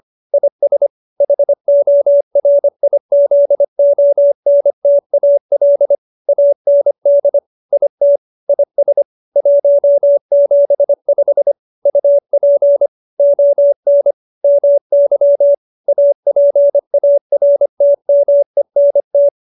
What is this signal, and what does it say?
Morse code, heard as a keyed tone, about 25 wpm, 570 Hz. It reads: IS HORIZONTAL AND IT IS 175 UP ON MY APARTMENT